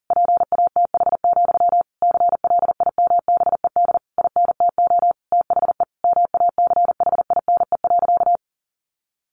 Morse code, heard as a keyed tone, about 40 wpm, 700 Hz. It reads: PATH, CLIMBED INTO THE MACHINE.
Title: A Sound Of Thunder 40wpm